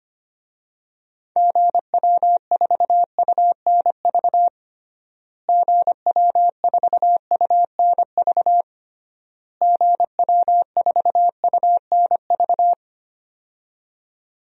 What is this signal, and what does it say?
Morse code, heard as a keyed tone, about 25 wpm, 700 Hz. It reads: GW4UNV GW4UNV GW4UNV
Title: DXCC-100 Call Signs - 6 Characters - Repeated 3x - 25wpm